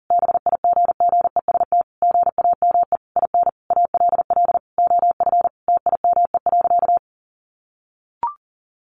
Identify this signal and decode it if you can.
Morse code, heard as a keyed tone, about 40 wpm, 700 Hz. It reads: BIGGEST GAME IN ALL OF TIME.
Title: A Sound Of Thunder 40wpm